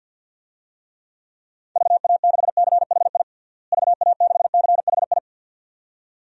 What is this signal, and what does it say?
Morse code, heard as a keyed tone, about 49 wpm, 700 Hz. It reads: VA6CHI VA6CHI
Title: DXCC-100 Call Signs - 6 Characters - Repeated 2x - 50wpm